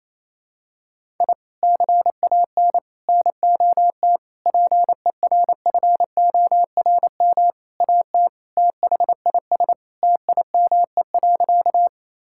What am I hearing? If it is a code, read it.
Morse code, heard as a keyed tone, about 28 wpm, 700 Hz. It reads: I CAN NOT PERFORM AT THIS TIME.
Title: Sentences from Top 1000 Words - Review Part 5 - 28wpm